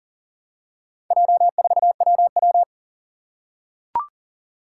Morse code, heard as a keyed tone, about 40 wpm, 690 Hz. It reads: J4WW
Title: Four Challenging Letter-Number Combinations 40wpm